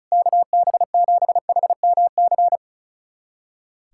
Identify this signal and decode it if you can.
Morse code, heard as a keyed tone, about 36 wpm, 700 Hz. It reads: KB7HMC
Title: USA Callsign DE Callsign - 6 Characters - 35wpm